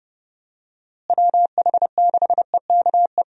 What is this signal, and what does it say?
Morse code, heard as a keyed tone, about 30 wpm, 700 Hz. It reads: WH6EKE
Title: USA Call Signs - 6 Characters - Repeated 2x - 30wpm